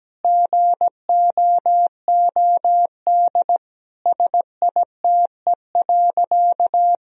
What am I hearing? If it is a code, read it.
Morse code, heard as a keyed tone, about 17 wpm, 700 Hz. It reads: GOOD SITE.
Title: Sentences from Top 1000 Words - Review Part 4 - 17wpm